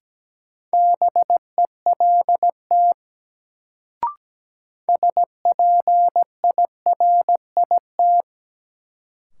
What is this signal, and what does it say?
Morse code, heard as a keyed tone, about 17 wpm, 700 Hz. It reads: BELT SPIRIT
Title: Sentences from Top 1400 Words - New Word Drill Alt - 17wpm